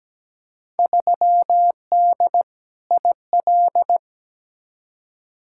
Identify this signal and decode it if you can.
Morse code, heard as a keyed tone, about 17 wpm, 700 Hz. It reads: 3D IL
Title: Field Day Exchanges Encore - 17wpm